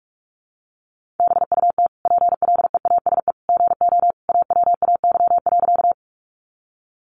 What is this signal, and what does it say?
Morse code, heard as a keyed tone, about 44 wpm, 700 Hz. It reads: BUT PLEASE GO AWAY.
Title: Sentences from Top 1500 Words 45wpm